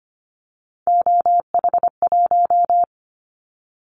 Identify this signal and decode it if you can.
Morse code, heard as a keyed tone, about 25 wpm, 700 Hz. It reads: OH1
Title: Callsign Prefixes for Top 50 DXCC - 25wpm